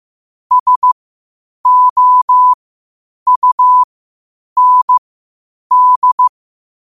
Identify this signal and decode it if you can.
Morse code, heard as a keyed tone, about 15 wpm, spaced as for 8 wpm, 1000 Hz. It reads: SOUND
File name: qotdfiles / 8wpm